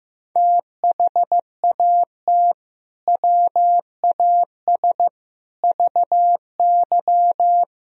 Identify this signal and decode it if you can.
Morse code, heard as a keyed tone, about 15 wpm, 700 Hz. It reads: THAT WAS VY
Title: Sentences from Top 200 Words and Abbreviations 15wpm